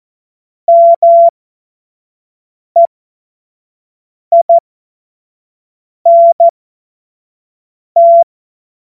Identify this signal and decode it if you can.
Morse code, heard as a keyed tone, about 14 wpm, spaced as for 5 wpm, 680 Hz. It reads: MEINT
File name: Morsekurs / Witze / Witz 20. Dezember 2022 25 F70